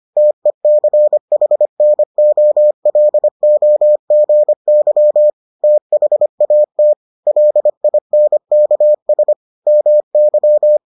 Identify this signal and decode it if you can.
Morse code, heard as a keyed tone, about 25 wpm, 600 Hz. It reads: TECHNOLOGY THAT LINKS MY